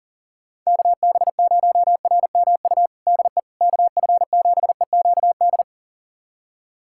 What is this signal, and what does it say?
Morse code, heard as a keyed tone, about 40 wpm, 700 Hz. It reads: KB0RMU DE KF7EQD
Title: USA Callsign DE Callsign - 6 Characters - 40wpm